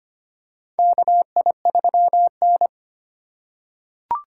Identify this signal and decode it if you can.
Morse code, heard as a keyed tone, about 25 wpm, 700 Hz. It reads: KI3N
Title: USA Call Signs - 4 Characters - 3x Character Spacing - 25wpm